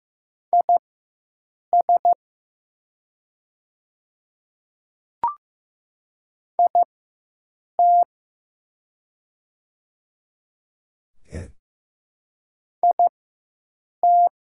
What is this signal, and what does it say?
Morse code, heard as a keyed tone, about 15 wpm, spaced as for 7 wpm, 700 Hz. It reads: IS IT IT
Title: 4x Character Spacing - Top 200 Words - 15wpm